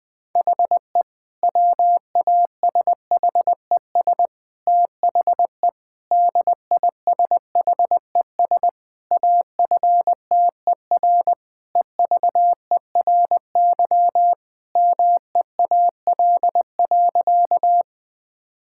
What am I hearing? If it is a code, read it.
Morse code, heard as a keyed tone, about 21 wpm, 700 Hz. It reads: HE WASHES THE DISHES AFTER EVERY MEAL.
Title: Difficult Sentences with Hints - 20wpm